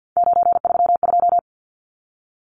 Morse code, heard as a keyed tone, about 50 wpm, 700 Hz. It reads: 932
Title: Numbers - 3-Digit 50wpm